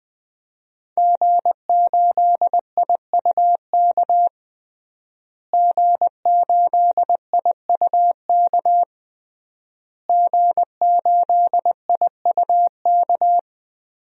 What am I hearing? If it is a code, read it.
Morse code, heard as a keyed tone, about 20 wpm, 700 Hz. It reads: G8IUK G8IUK G8IUK
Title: DXCC-100 Call Signs - 5 Characters - Repeated 3x - 20wpm